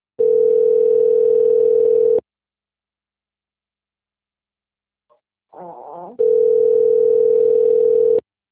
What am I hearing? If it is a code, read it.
Ringback tone (North America), 440 + 480 Hz.
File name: out- - - - - .70291